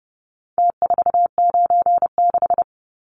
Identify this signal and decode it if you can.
Morse code, heard as a keyed tone, about 30 wpm, 700 Hz. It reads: T496
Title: Numbers - Contest Style - 4-Digit 30wpm